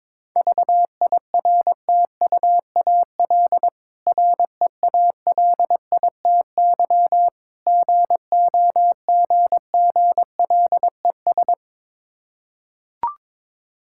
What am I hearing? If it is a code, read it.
Morse code, heard as a keyed tone, about 22 wpm, 690 Hz. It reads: VIRTUAL REALITY GOGGLES
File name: Difficult Words with Hints - 22wpm